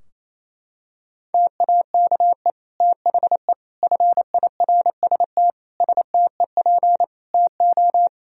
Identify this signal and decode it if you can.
Morse code, heard as a keyed tone, about 28 wpm, 700 Hz. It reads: TAKE THE FIRST STEP TO
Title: Sentences from Top 1000 Words - Review Part 4 - 28wpm